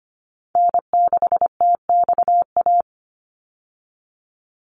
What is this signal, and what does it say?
Morse code, heard as a keyed tone, about 25 wpm, 690 Hz. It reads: N6TXA
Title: USA Call Signs 25wpm